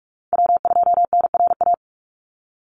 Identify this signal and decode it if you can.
Morse code, heard as a keyed tone, about 44 wpm, 700 Hz. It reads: W2NRA
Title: Call Signs Repeated 3x - 45wpm